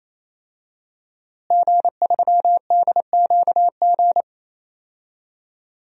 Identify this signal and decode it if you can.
Morse code, heard as a keyed tone, about 28 wpm, 700 Hz. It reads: G3DQG
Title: DXCC-100 Call Signs - 5 Characters - Repeated 3x - 28wpm